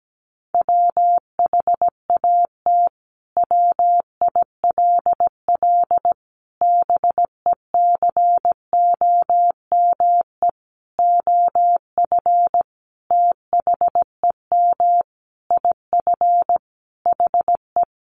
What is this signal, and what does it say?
Morse code, heard as a keyed tone, about 17 wpm, 700 Hz. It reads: WHAT WILL BECOME OF THEM IF HE
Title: Sentences from Top 1100 Words 17wpm